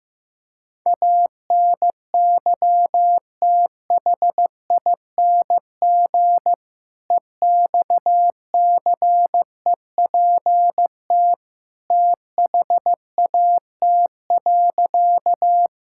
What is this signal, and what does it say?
Morse code, heard as a keyed tone, about 15 wpm, 700 Hz. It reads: ANYTHING EXCEPT THAT.
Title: Sentences from Top 1600 Words 15wpm